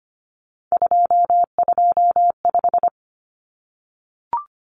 Morse code, heard as a keyed tone, about 25 wpm, 700 Hz. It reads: 225
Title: Common Cut Numbers - 3-Digit 25wpm